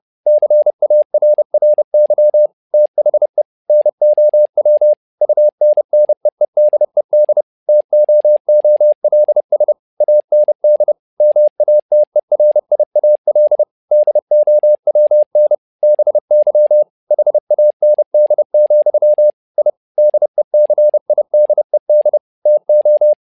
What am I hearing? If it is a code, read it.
Morse code, heard as a keyed tone, about 30 wpm, 600 Hz. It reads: CARRY THE NOW UNNEEDED TOOLS AND MATERIAL DOWN BY HAND, I DECIDED TO
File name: learning center / tower trouble / tower trouble 30wpm